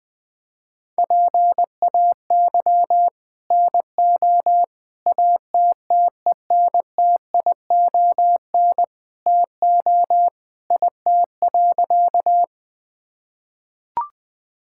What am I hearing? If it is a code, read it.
Morse code, heard as a keyed tone, about 21 wpm, 700 Hz. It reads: PAY NO ATTENTION TO IT.
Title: Sentences from Top 1000 Words - Review Part 6 - 20wpm